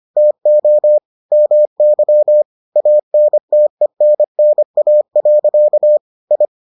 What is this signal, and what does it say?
Morse code, heard as a keyed tone, about 25 wpm, 600 Hz. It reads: TO MY ANTENNA. I